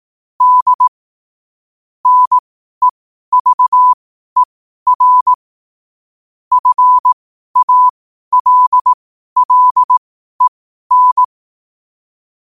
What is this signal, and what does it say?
Morse code, heard as a keyed tone, about 18 wpm, spaced as for 12 wpm, 1000 Hz. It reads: D NEVER FALLEN
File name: qotdfiles / 12wpm